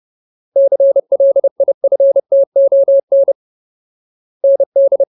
Morse code, heard as a keyed tone, about 30 wpm, 550 Hz. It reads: CLIFTON ND
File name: Ultimate Word List / 30WPM Ultimate